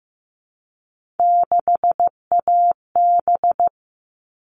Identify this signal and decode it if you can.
Morse code, heard as a keyed tone, about 15 wpm, 700 Hz. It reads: 6AB
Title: All Cut Numbers - 3-Digit 15wpm